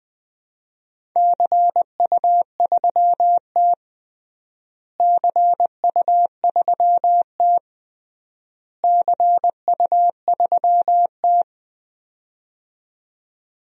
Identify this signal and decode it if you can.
Morse code, heard as a keyed tone, about 21 wpm, 700 Hz. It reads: CU3T CU3T CU3T
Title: DXCC-100 Call Signs - 4 Characters - Repeated 3x - 20wpm